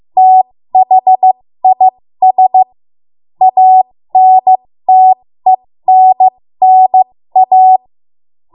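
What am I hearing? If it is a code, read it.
Morse code, heard as a keyed tone, about 15 wpm, spaced as for 13 wpm, 750 Hz. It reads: THIS ANTENNA
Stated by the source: 13 wpm Morse code practice archive transmission.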